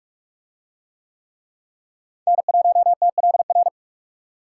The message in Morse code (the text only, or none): N1TLR